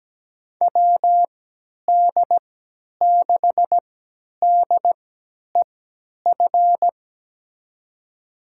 Morse code, heard as a keyed tone, about 17 wpm, spaced as for 10 wpm, 700 Hz. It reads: WD6DEF
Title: USA Call Signs - 6 Characters - 3x Character Spacing - 17wpm